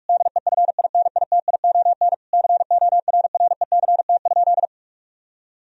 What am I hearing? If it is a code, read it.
Morse code, heard as a keyed tone, about 46 wpm, 700 Hz. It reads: DEFINITION CORRECT?
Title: Sentences from Top 1000 Words - Review Part 4 - 45wpm